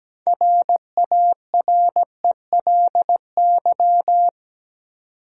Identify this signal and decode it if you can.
Morse code, heard as a keyed tone, about 17 wpm, 700 Hz. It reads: RARELY
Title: Six Letter Words - 17wpm